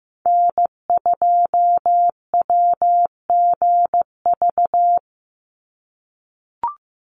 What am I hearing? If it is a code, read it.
Morse code, heard as a keyed tone, about 15 wpm, 700 Hz. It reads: N2WGV
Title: USA Call Signs 15wpm